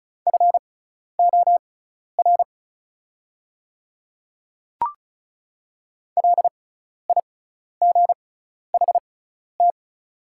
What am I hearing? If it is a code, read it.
Morse code, heard as a keyed tone, about 34 wpm, spaced as for 12 wpm, 690 Hz. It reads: FOR LIGHT
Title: 6x Character Spacing - Top 300 Words - 35wpm